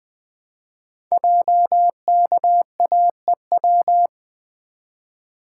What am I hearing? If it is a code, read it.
Morse code, heard as a keyed tone, about 20 wpm, 700 Hz. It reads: JKAEW
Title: Code Groups - 5 Letters - 20wpm